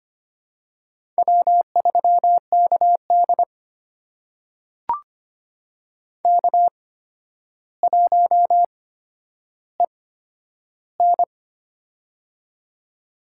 Morse code, heard as a keyed tone, about 25 wpm, spaced as for 11 wpm, 700 Hz. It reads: W3KD K1EN
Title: USA Call Signs - 4 Characters - 8x Character Spacing - 25wpm